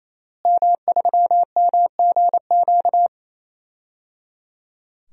Morse code, heard as a keyed tone, about 28 wpm, 700 Hz. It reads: M3MGQ
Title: DXCC-100 Call Signs - 5 Characters - Repeated 3x - 28wpm